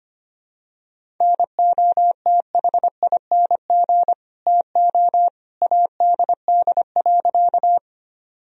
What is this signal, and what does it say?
Morse code, heard as a keyed tone, about 25 wpm, 700 Hz. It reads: NOTHING TO ADD.
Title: Sentences from Top 1000 Words - Review Part 1 - 25wpm